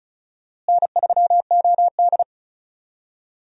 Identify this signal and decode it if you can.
Morse code, heard as a keyed tone, about 36 wpm, 700 Hz. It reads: N3OD
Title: USA Call Signs - 4 Characters - Repeated 2x - 35wpm